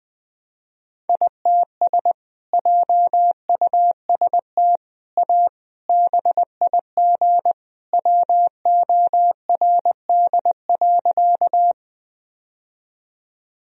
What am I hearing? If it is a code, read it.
Morse code, heard as a keyed tone, about 20 wpm, 700 Hz. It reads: ITS JUST A BIG WORD.
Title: Sentences from Top 1000 Words - Review Part 1 - 20wpm